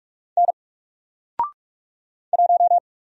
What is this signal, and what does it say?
Morse code, heard as a keyed tone, about 44 wpm, 700 Hz. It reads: N 1
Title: Single Letter-Number 45wpm